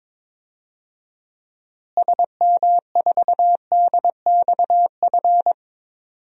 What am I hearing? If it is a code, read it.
Morse code, heard as a keyed tone, about 22 wpm, 700 Hz. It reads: SM4DXF
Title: DXCC-100 Call Signs - 6 Characters - Repeated 2x - 22wpm